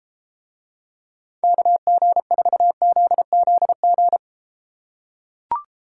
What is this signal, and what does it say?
Morse code, heard as a keyed tone, about 31 wpm, 700 Hz. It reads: KG4ZZG